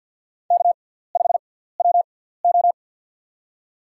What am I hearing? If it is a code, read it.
Morse code, heard as a keyed tone, about 53 wpm, spaced as for 17 wpm, 700 Hz. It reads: K5WO
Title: USA Call Signs - 4 Characters - 6x Character Spacing - 50wpm